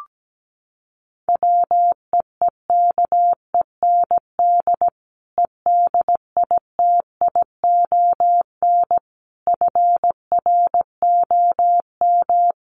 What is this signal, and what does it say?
Morse code, heard as a keyed tone, about 17 wpm, 700 Hz. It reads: WEEKEND EDITION FROM
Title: Sets of 5 Words 17wpm